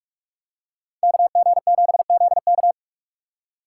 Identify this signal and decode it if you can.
Morse code, heard as a keyed tone, about 42 wpm, 700 Hz. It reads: KG7ZK